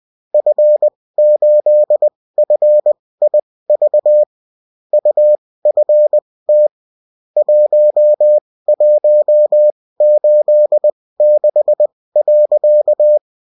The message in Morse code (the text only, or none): F8FIV UFT 1186.